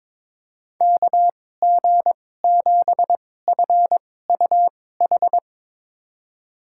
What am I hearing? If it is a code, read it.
Morse code, heard as a keyed tone, about 22 wpm, spaced as for 16 wpm, 700 Hz. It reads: KG7FUH
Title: USA Call Signs - 6 Characters - 2x Character Spacing - 22wpm